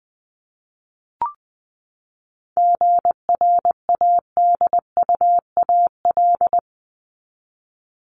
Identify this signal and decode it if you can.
Morse code, heard as a keyed tone, about 20 wpm, 700 Hz. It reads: GRADUAL
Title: LY Suffix - Top 100 Words with Base Words 20wpm